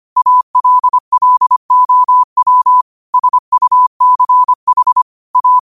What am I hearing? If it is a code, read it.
Morse code, heard as a keyed tone, about 25 wpm, 1000 Hz. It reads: ALLOW SUCH A
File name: qotdfiles / 25wpm